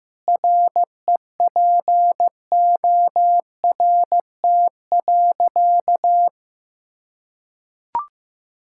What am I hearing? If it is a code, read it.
Morse code, heard as a keyed tone, about 14 wpm, 700 Hz. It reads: REPORT.
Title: Sentences from Top 1000 Words - Review Part 6 - 15wpm